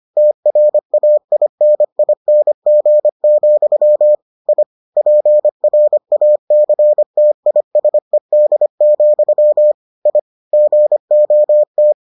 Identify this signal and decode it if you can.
Morse code, heard as a keyed tone, about 25 wpm, 600 Hz. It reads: TRAINING, I PRACTISED, I GOT